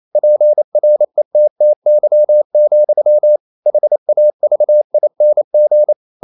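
Morse code, heard as a keyed tone, about 28 wpm, 600 Hz. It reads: PRETTY, HAVING
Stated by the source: Morse code practice at 28 wpm, text of Ken Follett's novel Never in English, from CW med Gnister.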